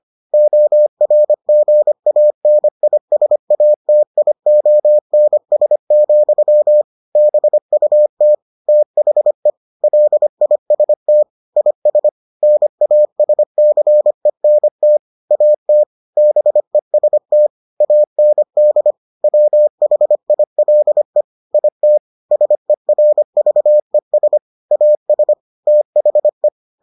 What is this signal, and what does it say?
Morse code, heard as a keyed tone, about 25 wpm, 600 Hz. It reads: ORGANISATIONS, BUT THE LIST IS NASCENT AT BEST AND WHILE IT SERVES AS THE